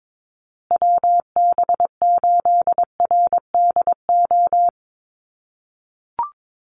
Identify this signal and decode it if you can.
Morse code, heard as a keyed tone, about 22 wpm, 700 Hz. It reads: WB8RDO
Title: USA Call Signs Repeated 2x 22wpm